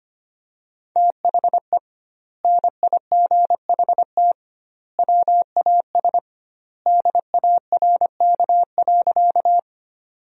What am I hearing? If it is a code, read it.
Morse code, heard as a keyed tone, about 25 wpm, 700 Hz. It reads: THE NIGHT WAS DARK.
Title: 2x Word-Spacing - Sentences from Top 500 Words - 25wpm